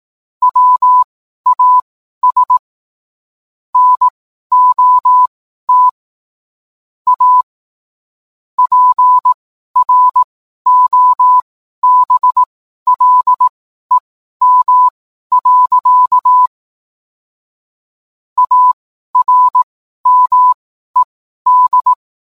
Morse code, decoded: WAS NOT A PROBLEM. ARMED

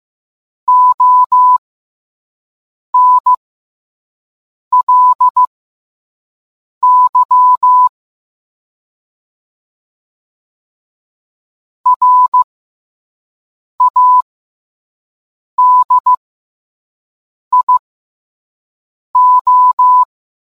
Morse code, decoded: ONLY RADIO

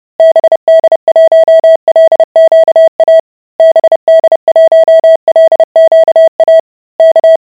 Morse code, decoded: BD1LQA BD1LQA K

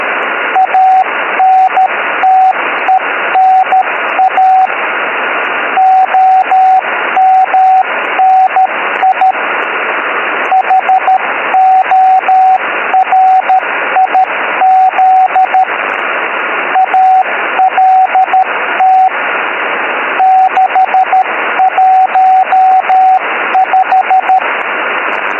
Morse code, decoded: ANTENA OMNI HORIZ ALT 615

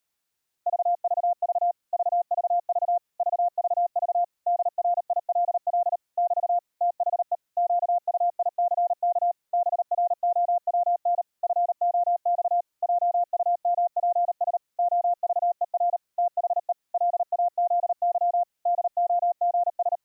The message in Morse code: VVV VVV VVV DRILL <BT> THE QUICK BROWN FOX JUMPS OVER THE LAZY DOGS

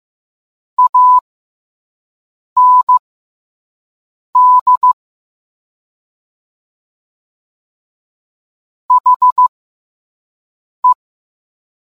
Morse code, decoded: AND HE

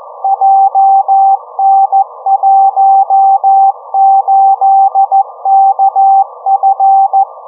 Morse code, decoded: JN18KF

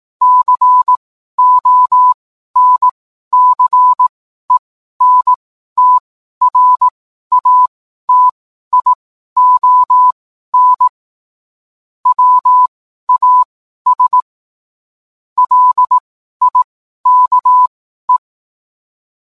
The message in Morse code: CONCENTRATION WAS LIKE

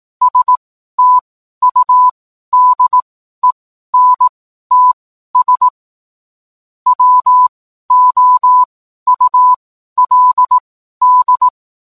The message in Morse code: STUDENTS WOULD

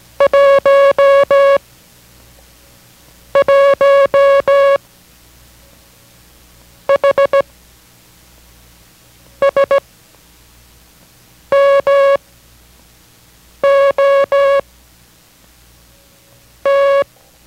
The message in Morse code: 1 1 H S M O T

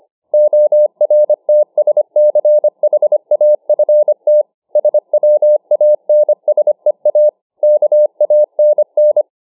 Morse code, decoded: ORTSCHAFT SWANSEA KANN

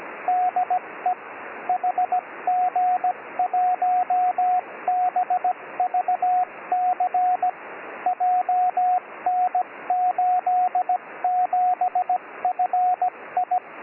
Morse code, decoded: DE HG1BVC JN87FI

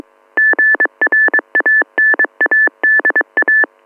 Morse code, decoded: GRANADA

